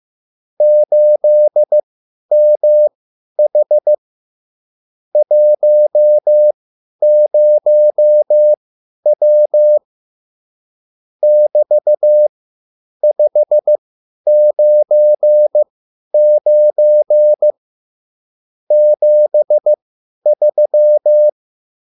Morse code, heard as a keyed tone, about 15 wpm, spaced as for 10 wpm, 600 Hz. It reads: 8MH 10W <BT>599 73